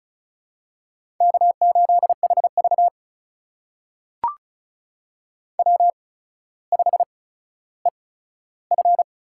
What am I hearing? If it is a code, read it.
Morse code, heard as a keyed tone, about 36 wpm, spaced as for 15 wpm, 700 Hz. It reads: K8HV W5EF
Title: USA Call Signs - 4 Characters - 8x Character Spacing - 35wpm